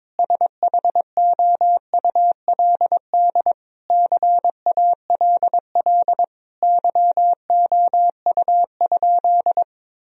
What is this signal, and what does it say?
Morse code, heard as a keyed tone, about 22 wpm, 700 Hz. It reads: SHOULD CALL YOU?
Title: Sentences from Top 1000 Words - Review Part 3 - 22wpm